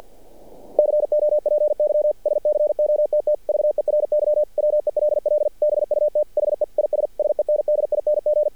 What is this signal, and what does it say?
Morse code, heard as a keyed tone, about 50 wpm, 600 Hz. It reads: COPY SCOTT VERY WELL BUT HE IS SENDING
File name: archived / CRSnet